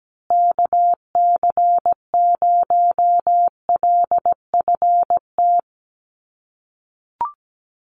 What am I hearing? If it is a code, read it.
Morse code, heard as a keyed tone, about 17 wpm, 700 Hz. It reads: KC0LFT
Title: USA Call Signs Repeated 2x 17wpm